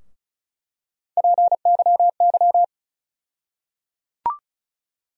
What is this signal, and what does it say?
Morse code, heard as a keyed tone, about 36 wpm, 700 Hz. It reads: PYY
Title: USA Call Sign Postfixes - 3 Letters 35wpm